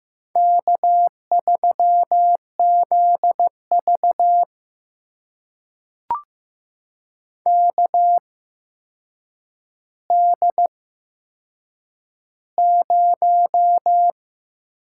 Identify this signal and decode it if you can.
Morse code, heard as a keyed tone, about 15 wpm, spaced as for 7 wpm, 700 Hz. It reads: K3ZV KD0
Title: USA Call Signs - 4 Characters - 8x Character Spacing - 15wpm